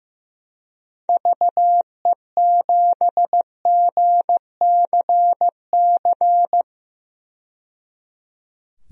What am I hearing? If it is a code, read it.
Morse code, heard as a keyed tone, about 16 wpm, 700 Hz. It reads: VE7GCC
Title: DXCC-100 Call Signs - 6 Characters - 15wpm